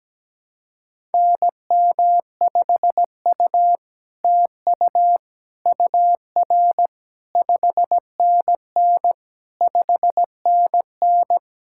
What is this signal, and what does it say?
Morse code, heard as a keyed tone, about 17 wpm, 700 Hz. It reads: NM5U TU UR 5NN 5NN
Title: SOTA QSO Practice 17wpm